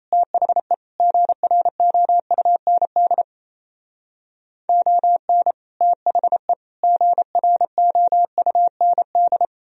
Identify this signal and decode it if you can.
Morse code, heard as a keyed tone, about 37 wpm, 700 Hz. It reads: THE GROUND ON THE GROUND